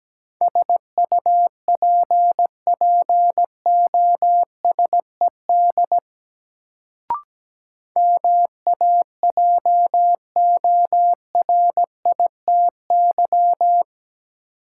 Morse code, heard as a keyed tone, about 17 wpm, 700 Hz. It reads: SUPPOSED MAJORITY
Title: Sets of 3 Words - New Word Drill - 17wpm